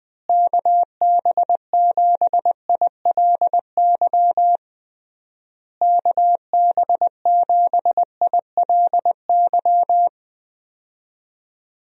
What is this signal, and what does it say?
Morse code, heard as a keyed tone, about 20 wpm, 700 Hz. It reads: KB7ILY KB7ILY
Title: USA Call Signs - 6 Characters - Repeated 3x - 20wpm